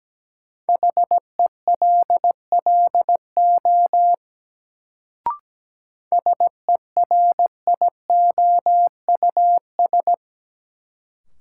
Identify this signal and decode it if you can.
Morse code, heard as a keyed tone, about 17 wpm, 700 Hz. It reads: HELLO SERIOUS
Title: Sentences from Top 1100 Words - New Word Drill - 17wpm